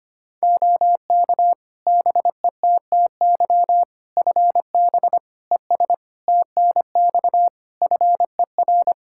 Morse code, heard as a keyed tone, about 25 wpm, 700 Hz. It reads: OK BETTY FB ES TNX FER
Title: Practice QSO - Exchange 2 - Info - 25wpm